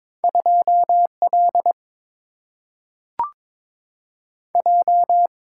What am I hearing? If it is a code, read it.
Morse code, heard as a keyed tone, about 21 wpm, 700 Hz. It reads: 2 L J
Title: Single-Letter Single-Number 22wpm